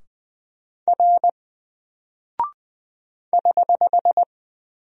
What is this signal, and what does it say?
Morse code, heard as a keyed tone, about 20 wpm, 700 Hz. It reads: R <HH>
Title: Single Character (Letter:Number:Punctuation:Prosign) 20wpm